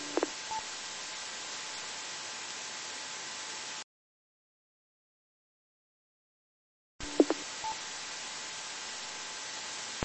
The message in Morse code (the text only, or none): none